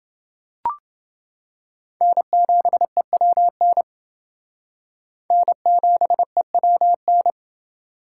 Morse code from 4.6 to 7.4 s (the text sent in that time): N7EWN